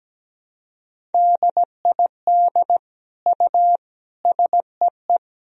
Morse code, heard as a keyed tone, about 17 wpm, 700 Hz. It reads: DID U SEE
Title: Sentences from Top 100 Words and Abbreviations 17wpm